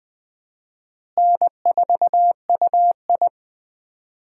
Morse code, heard as a keyed tone, about 20 wpm, 700 Hz. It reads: N4UI
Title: USA Call Signs - 4 Characters - Repeated 2x - 20wpm